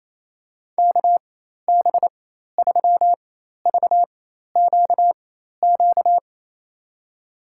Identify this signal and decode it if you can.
Morse code, heard as a keyed tone, about 27 wpm, spaced as for 13 wpm, 700 Hz. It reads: KB3VQQ
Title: USA Call Signs - 4x Character Spacing - 28wpm